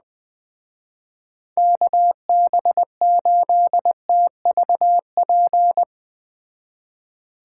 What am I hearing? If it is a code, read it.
Morse code, heard as a keyed tone, about 20 wpm, 700 Hz. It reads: KB8TVP
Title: Speed Racing - USA Call Signs 17wpm